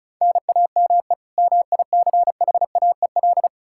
Morse code, heard as a keyed tone, about 36 wpm, 700 Hz. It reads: NAME MICHAEL
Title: Practice QSO - Exchange 1 - Beginning - 35wpm